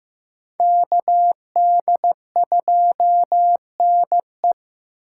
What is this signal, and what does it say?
Morse code, heard as a keyed tone, about 18 wpm, 700 Hz. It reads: KD2NE
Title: Speed Racing - USA 2x2 Call Signs 15wpm